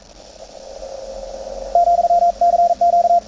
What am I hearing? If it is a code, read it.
Morse code, heard as a keyed tone, about 41 wpm, 690 Hz, with radio noise. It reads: , CQ
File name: attachments / CW 20m 200Hz